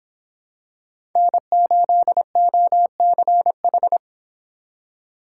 Morse code, heard as a keyed tone, about 23 wpm, 700 Hz. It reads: N8OCH